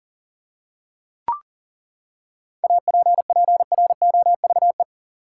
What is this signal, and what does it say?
Morse code, heard as a keyed tone, about 39 wpm, 700 Hz. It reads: APPROVE
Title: Seven Letter Words - 40wpm